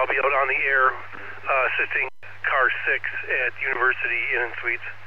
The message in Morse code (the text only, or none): none